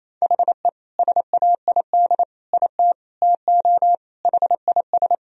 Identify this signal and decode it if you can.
Morse code, heard as a keyed tone, about 27 wpm, 700 Hz. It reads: HE SAID IT TO HIS
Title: Sentences from Top 1000 Words - Review Part 1 - 28wpm